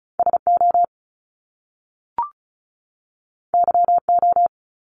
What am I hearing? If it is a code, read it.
Morse code, heard as a keyed tone, about 36 wpm, 700 Hz. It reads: SO YO
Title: Top Two Letter Combinations in Words 35wpm